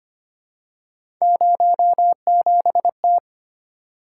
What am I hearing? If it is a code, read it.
Morse code, heard as a keyed tone, about 25 wpm, 700 Hz. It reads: 07T
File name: Three Letter-Number Combinations - 25wpm